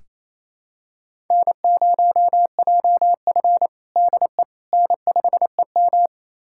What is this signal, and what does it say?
Morse code, heard as a keyed tone, about 28 wpm, 700 Hz. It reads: N0JF DE N5EM
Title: USA Callsign DE Callsign - 4 Characters - 28wpm